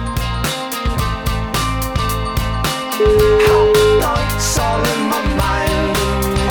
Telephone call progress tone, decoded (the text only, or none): ringback tone